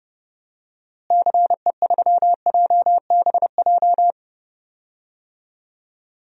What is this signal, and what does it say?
Morse code, heard as a keyed tone, about 32 wpm, 700 Hz. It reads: CE3JBJ
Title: DXCC-100 Call Signs - 6 Characters - Repeated 2x - 30wpm